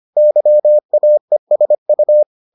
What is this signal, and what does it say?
Morse code, heard as a keyed tone, about 25 wpm, 600 Hz. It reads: YAESU